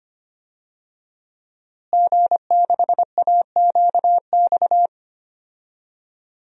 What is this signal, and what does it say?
Morse code, heard as a keyed tone, about 25 wpm, 700 Hz. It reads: G6AQX
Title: DXCC-100 Call Signs - 5 Characters - Repeated 3x - 25wpm